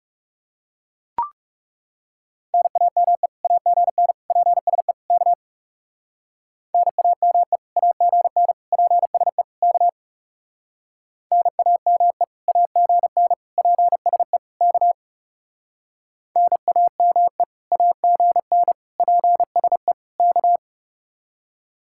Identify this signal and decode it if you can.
Morse code, heard as a keyed tone, about 37 wpm, 700 Hz. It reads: NAME AGN PSE K NAME AGN PSE K NAME AGN PSE K NAME AGN PSE K
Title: Speed-Racing Common QSO Phrases - 30wpm